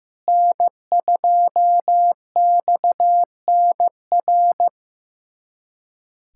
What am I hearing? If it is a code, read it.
Morse code, heard as a keyed tone, about 15 wpm, 700 Hz. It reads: N2XNR
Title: USA Callsign DE Callsign - 5 Characters - 15wpm